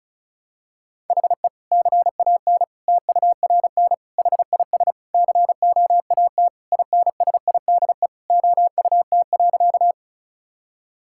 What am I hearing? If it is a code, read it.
Morse code, heard as a keyed tone, about 36 wpm, 700 Hz. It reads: HE CAN TURN HIS COAT INSIDE OUT.
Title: Sentences from Top 1600 Words 35wpm